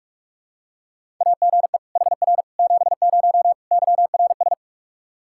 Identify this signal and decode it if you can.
Morse code, heard as a keyed tone, about 46 wpm, 700 Hz. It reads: AGE HR 70 YRS
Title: Practice QSO - Exchange 3 - More Info - 45wpm